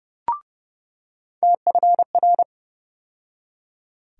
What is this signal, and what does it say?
Morse code, heard as a keyed tone, about 32 wpm, 700 Hz. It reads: TFR
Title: USA Call Sign Postfixes - 3 Letters 30wpm